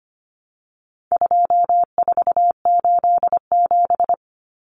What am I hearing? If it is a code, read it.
Morse code, heard as a keyed tone, about 25 wpm, 700 Hz. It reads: 2487
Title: Numbers - 4-Digit 25wpm